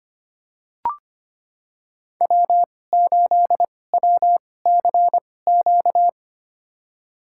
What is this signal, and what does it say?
Morse code, heard as a keyed tone, about 25 wpm, spaced as for 18 wpm, 700 Hz. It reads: W8WCQ
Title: USA Call Signs - 5 Characters - 2x Character Spacing - 25wpm